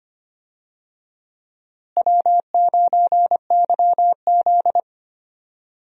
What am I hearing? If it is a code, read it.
Morse code, heard as a keyed tone, about 25 wpm, 700 Hz. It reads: W9YZ
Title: USA Call Signs - 4 Characters - Repeated 3x - 25wpm